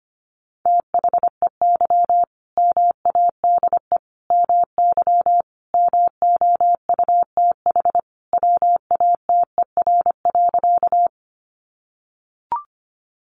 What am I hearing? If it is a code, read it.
Morse code, heard as a keyed tone, about 25 wpm, 700 Hz. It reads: THEY MADE MY MOUTH WATER.
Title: Sentences from Top 1500 Words 25wpm